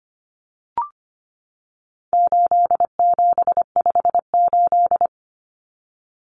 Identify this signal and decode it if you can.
Morse code, heard as a keyed tone, about 25 wpm, 700 Hz. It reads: 8758
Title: Numbers - 4-Digit 25wpm